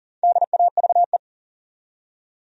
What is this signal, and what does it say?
Morse code, heard as a keyed tone, about 38 wpm, 700 Hz. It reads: DAVE
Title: USA Common Names - Max 4 Letters - 40wpm